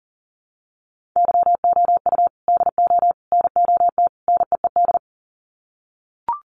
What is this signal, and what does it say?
Morse code, heard as a keyed tone, about 40 wpm, 700 Hz. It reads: YOU DO NOT NEED
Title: Sets of 4 Words 40wpm